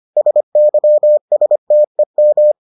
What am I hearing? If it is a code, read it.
Morse code, heard as a keyed tone, about 25 wpm, 600 Hz. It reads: SYSTEM